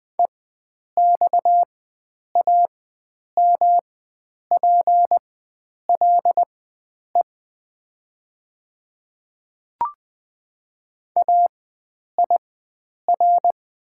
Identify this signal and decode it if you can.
Morse code, heard as a keyed tone, about 20 wpm, spaced as for 9 wpm, 700 Hz. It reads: EXAMPLE AIR
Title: 4x Character Spacing - Top 500 Words - 20wpm